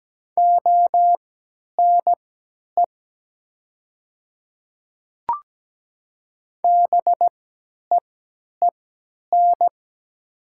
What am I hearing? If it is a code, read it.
Morse code, heard as a keyed tone, about 17 wpm, spaced as for 10 wpm, 700 Hz. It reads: ONE BEEN
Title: 3x Character Spacing - Top 100 Words - 17wpm